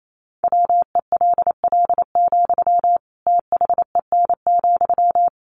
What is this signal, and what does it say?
Morse code, heard as a keyed tone, about 28 wpm, 700 Hz. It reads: WELL, THEN,
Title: Sentences from Top 1200 Words 28wpm